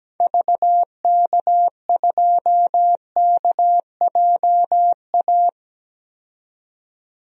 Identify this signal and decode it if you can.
Morse code, heard as a keyed tone, about 17 wpm, 700 Hz. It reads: VK2KJA
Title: DXCC-100 Call Signs - 6 Characters - 17wpm